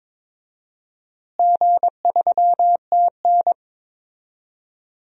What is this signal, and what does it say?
Morse code, heard as a keyed tone, about 22 wpm, 700 Hz. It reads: G3TN
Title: DXCC-100 Call Signs - 4 Characters - Repeated 3x - 22wpm